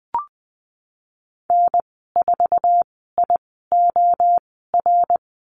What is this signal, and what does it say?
Morse code, heard as a keyed tone, about 19 wpm, spaced as for 14 wpm, 700 Hz. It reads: N4IOR
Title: 088 Lesson 15g - Number 5 - Call Signs 20wpm